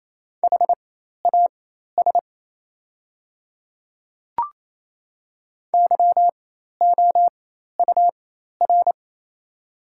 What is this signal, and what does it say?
Morse code, heard as a keyed tone, about 28 wpm, spaced as for 13 wpm, 700 Hz. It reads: HAS YOUR
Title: 4x Character Spacing - Top 100 Words - 28wpm